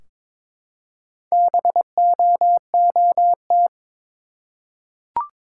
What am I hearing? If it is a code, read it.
Morse code, heard as a keyed tone, about 22 wpm, 700 Hz. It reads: BOOT
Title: Four Letter Words - 22wpm